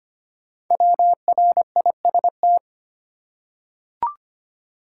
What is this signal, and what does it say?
Morse code, heard as a keyed tone, about 25 wpm, 700 Hz. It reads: WRIST
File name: 5 Letter Words - First 3 Letters Given - 25wpm